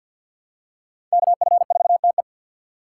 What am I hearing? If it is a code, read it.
Morse code, heard as a keyed tone, about 53 wpm, 690 Hz. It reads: KF4TE
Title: USA Call Signs - 5 Characters - 3x Character Spacing - 50wpm